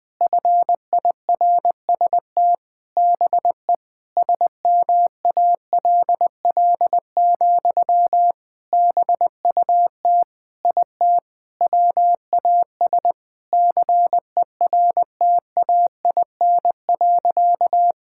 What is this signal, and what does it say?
Morse code, heard as a keyed tone, about 20 wpm, 700 Hz. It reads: FIRST BE SMALL, BUT IT WAS CERTAIN.
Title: Sentences from Top 1000 Words - Review Part 6 - 20wpm